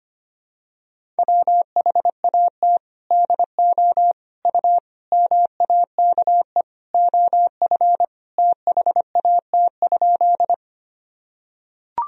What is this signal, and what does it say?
Morse code, heard as a keyed tone, about 25 wpm, 700 Hz. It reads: WHAT DO U MAKE OF THAT?
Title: Sentences from Top 100 Words and Abbreviations 25wpm